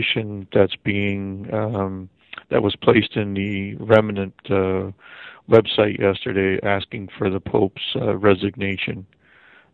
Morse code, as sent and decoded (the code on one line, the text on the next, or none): none